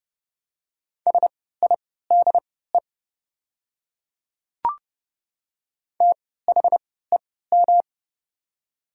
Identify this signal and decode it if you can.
Morse code, heard as a keyed tone, about 30 wpm, spaced as for 17 wpm, 700 Hz. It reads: SIDE THEM
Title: 3x Character Spacing - Top 300 Words - 30wpm